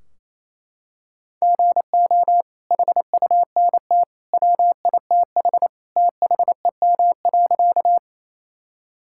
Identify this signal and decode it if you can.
Morse code, heard as a keyed tone, about 28 wpm, 700 Hz. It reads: GO HUNT WITH THEM.
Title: Sentences from Top 1600 Words 28wpm